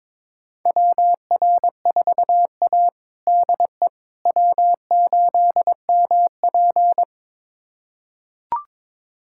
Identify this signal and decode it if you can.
Morse code, heard as a keyed tone, about 22 wpm, 700 Hz. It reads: WR4A DE W8MP
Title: USA Callsign DE Callsign - 4 Characters - 22wpm